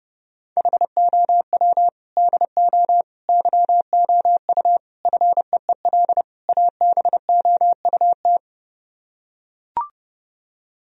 Morse code, heard as a keyed tone, about 32 wpm, 700 Hz. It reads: HOW DO YOU FEEL ABOUT